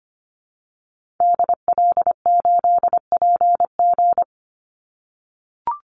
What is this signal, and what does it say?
Morse code, heard as a keyed tone, about 25 wpm, 700 Hz. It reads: DL8PG
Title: Call Signs - 25wpm